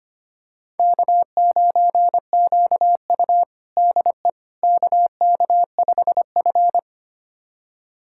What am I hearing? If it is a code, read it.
Morse code, heard as a keyed tone, about 25 wpm, 700 Hz. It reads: K9QU DE KK5F
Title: USA Callsign DE Callsign - 4 Characters - 25wpm